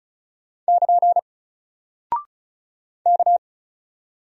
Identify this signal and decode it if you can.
Morse code, heard as a keyed tone, about 36 wpm, 700 Hz. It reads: <KN> K
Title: Single Character (Letter:Number:Punctuation:Prosign) Alt 35wpm